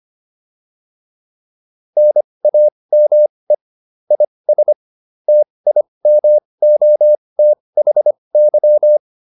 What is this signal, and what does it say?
Morse code, heard as a keyed tone, about 25 wpm, spaced as for 20 wpm, 600 Hz. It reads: NAME IS TIMOTHY